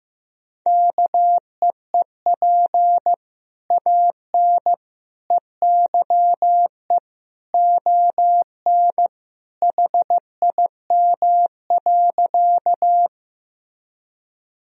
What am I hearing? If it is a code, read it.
Morse code, heard as a keyed tone, about 15 wpm, 700 Hz. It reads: KEEP AN EYE ON HIM.
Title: Sentences from Top 1000 Words - Review Part 1 - 15wpm